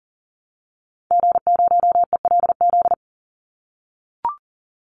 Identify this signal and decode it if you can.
Morse code, heard as a keyed tone, about 40 wpm, 700 Hz. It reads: G0ELZ
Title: Call Signs - 40wpm